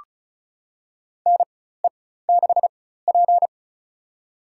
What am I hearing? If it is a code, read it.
Morse code, heard as a keyed tone, about 36 wpm, spaced as for 16 wpm, 700 Hz. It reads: NE6P
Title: USA Call Signs - 4 Characters - 4x Character Spacing - 35wpm